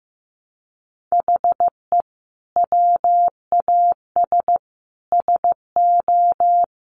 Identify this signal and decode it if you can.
Morse code, heard as a keyed tone, about 15 wpm, 700 Hz. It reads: HE WAS SO
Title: Sentences from Top 200 Words 15wpm